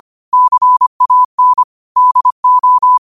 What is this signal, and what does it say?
Morse code, heard as a keyed tone, about 25 wpm, 1000 Hz. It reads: CAN DO
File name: qotdfiles / 25wpm